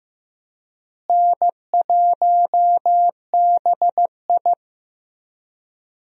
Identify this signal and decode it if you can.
Morse code, heard as a keyed tone, about 15 wpm, 700 Hz. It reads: N1BI
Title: USA Call Signs - 4 Characters - Repeated 2x - 15wpm